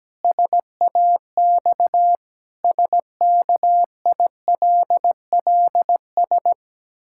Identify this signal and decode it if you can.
Morse code, heard as a keyed tone, about 17 wpm, 700 Hz. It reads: SAX SKILLS
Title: Pangram Challenge v2 - 17wpm